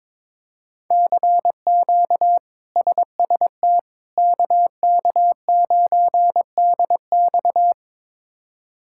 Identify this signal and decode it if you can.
Morse code, heard as a keyed tone, about 22 wpm, 700 Hz. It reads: CQ SST KK9DX
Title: K1USN SST QSO Practice - 22wpm